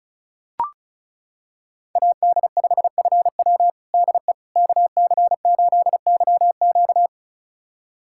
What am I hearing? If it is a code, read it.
Morse code, heard as a keyed tone, about 36 wpm, 700 Hz. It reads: AD5FW DE KC8YQ
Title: USA Callsign DE Callsign - 5 Characters - 35wpm